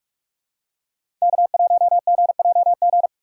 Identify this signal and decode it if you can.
Morse code, heard as a keyed tone, about 45 wpm, 700 Hz. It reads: K1GJG
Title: USA Call Signs - 5 Characters - 6x Character Spacing - 45wpm